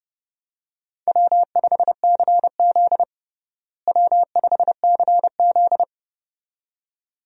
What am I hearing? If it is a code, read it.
Morse code, heard as a keyed tone, about 29 wpm, 700 Hz. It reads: W5CZ W5CZ
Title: USA Call Signs - 4 Characters - Repeated 2x - 30wpm